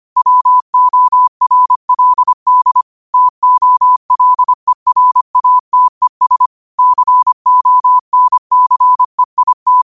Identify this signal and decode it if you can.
Morse code, heard as a keyed tone, about 25 wpm, 1000 Hz. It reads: WORLD TOLERATES CONCEIT
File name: qotdfiles / 25wpm